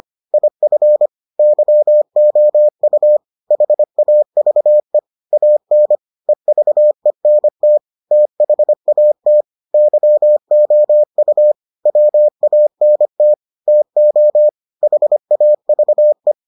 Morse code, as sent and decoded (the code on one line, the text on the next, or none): .. ..-. / -.-- --- ..- / .... .- ...- . / .- -. / . ...- . -. - / - .... .- - / -.-- --- ..- / .-- .- -. - / - --- / .... .- ...- .
IF YOU HAVE AN EVENT THAT YOU WANT TO HAVE